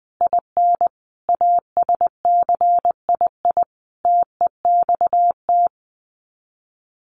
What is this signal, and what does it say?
Morse code, heard as a keyed tone, about 20 wpm, 700 Hz. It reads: IN ASCII TEXT
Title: Sets of 5 Words 20wpm